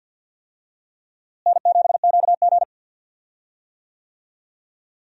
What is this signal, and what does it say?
Morse code, heard as a keyed tone, about 47 wpm, 700 Hz. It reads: N7QG
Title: USA Call Signs - 4 Characters - Repeated 2x - 50wpm